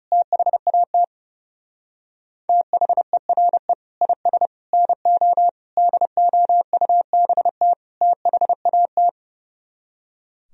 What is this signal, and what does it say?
Morse code, heard as a keyed tone, about 39 wpm, 700 Hz. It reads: THAT THERE IS NO DOUBT THAT